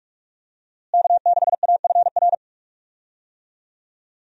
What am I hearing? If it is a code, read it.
Morse code, heard as a keyed tone, about 46 wpm, 700 Hz. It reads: K6AFR
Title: USA Call Signs - 5 Characters - Repeated 3x - 45wpm